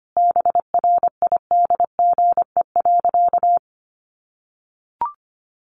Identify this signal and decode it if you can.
Morse code, heard as a keyed tone, about 25 wpm, 700 Hz. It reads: BRIDGE.
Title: Sentences from Top 1500 Words 25wpm